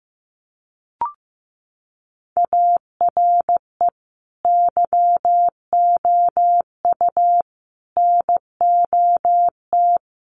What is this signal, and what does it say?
Morse code, heard as a keyed tone, about 14 wpm, 700 Hz. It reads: ARE YOU NOT
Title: Sentences from Top 1000 Words 15wpm